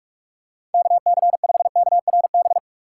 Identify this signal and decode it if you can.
Morse code, heard as a keyed tone, about 43 wpm, 690 Hz. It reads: KC5KRB
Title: USA Call Signs - 6 Characters - 2x Character Spacing - 45wpm